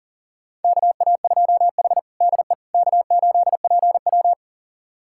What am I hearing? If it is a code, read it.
Morse code, heard as a keyed tone, about 37 wpm, 700 Hz. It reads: KA2H DE K8PW
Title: USA Callsign DE Callsign - 4 Characters - 40wpm